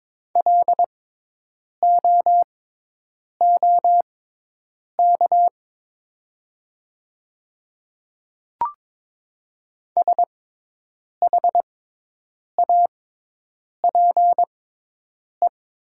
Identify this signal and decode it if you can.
Morse code, heard as a keyed tone, about 22 wpm, spaced as for 8 wpm, 700 Hz. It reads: LOOK SHAPE
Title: 6x Character Spacing - Top 500 Words - 22wpm